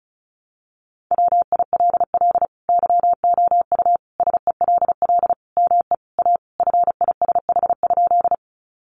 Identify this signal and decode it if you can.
Morse code, heard as a keyed tone, about 34 wpm, 700 Hz. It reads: WILL YOU SELL ME A FISH?
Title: Sentences from Top 1100 Words 35wpm